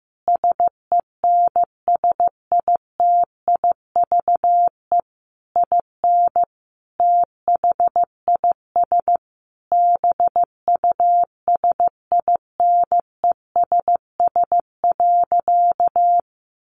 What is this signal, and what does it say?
Morse code, heard as a keyed tone, about 15 wpm, 700 Hz. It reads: SENSITIVE IN THIS BUSINESS.
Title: Sentences from Top 1400 Words 15wpm